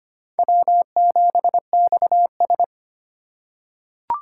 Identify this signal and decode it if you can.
Morse code, heard as a keyed tone, about 25 wpm, 700 Hz. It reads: W7XS
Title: USA Call Signs - 4 Characters - 2x Character Spacing - 25wpm